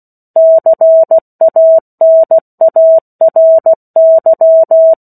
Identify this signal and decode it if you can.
Morse code, heard as a keyed tone, about 16 wpm, 650 Hz. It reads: CANARY
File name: 03-Chapter-03 16 WPM